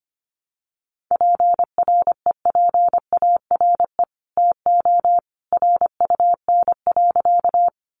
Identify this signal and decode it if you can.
Morse code, heard as a keyed tone, about 25 wpm, 700 Hz. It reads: PREPARE TO RUN.
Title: Sentences from Top 900 Words 25wpm